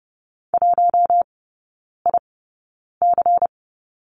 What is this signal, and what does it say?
Morse code, heard as a keyed tone, about 30 wpm, spaced as for 9 wpm, 690 Hz. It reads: 1IC
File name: Lesson 16e - Letter c - Three Characters 30wpm